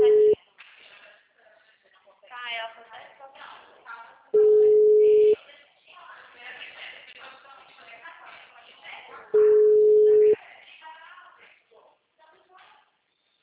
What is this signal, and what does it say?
Ringback tone (Europe, 425 Hz), with speech.